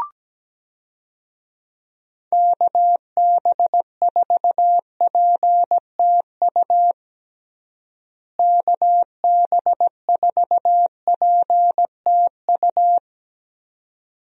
Morse code, heard as a keyed tone, about 17 wpm, 700 Hz. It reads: KB4PTU KB4PTU
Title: USA Call Signs - 6 Characters - Repeated 2x - 17wpm